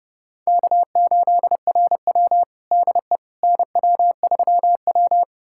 Morse code, heard as a keyed tone, about 32 wpm, 700 Hz. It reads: K8RW DE NW3W
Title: USA Callsign DE Callsign - 4 Characters - 30wpm